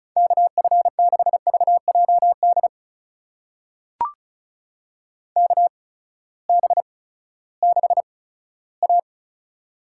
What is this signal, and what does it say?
Morse code, heard as a keyed tone, about 36 wpm, spaced as for 18 wpm, 700 Hz. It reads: KF6VJD KB6A
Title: USA Call Signs - 6 Characters - 8x Character Spacing - 35wpm